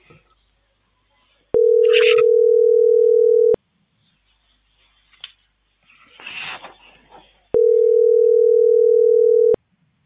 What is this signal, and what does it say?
Ringback tone (North America), 440 + 480 Hz.